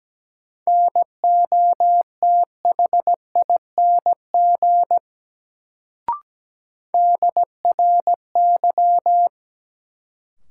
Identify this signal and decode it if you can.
Morse code, heard as a keyed tone, about 17 wpm, 700 Hz. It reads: NOTHING DRY
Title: Sentences from Top 500 Words - New Word Drill - 17wpm